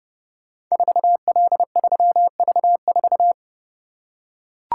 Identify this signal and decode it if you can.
Morse code, heard as a keyed tone, about 30 wpm, 700 Hz. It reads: 4L3V4
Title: Code Groups - 5 Letter:Numbers - 30wpm